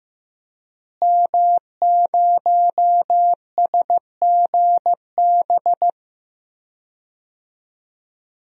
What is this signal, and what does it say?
Morse code, heard as a keyed tone, about 15 wpm, 700 Hz. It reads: M0SGB
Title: DXCC-100 Call Signs - 5 Characters - Repeated 3x - 15wpm